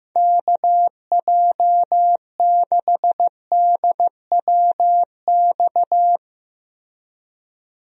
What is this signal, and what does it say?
Morse code, heard as a keyed tone, about 15 wpm, 700 Hz. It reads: KJ6DWX
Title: USA Call Signs - 6 Characters - Repeated 3x - 15wpm